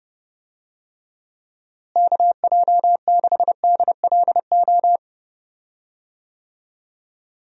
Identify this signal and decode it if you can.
Morse code, heard as a keyed tone, about 32 wpm, 700 Hz. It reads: KJ6DLO
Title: USA Call Signs - 6 Characters - Repeated 3x - 30wpm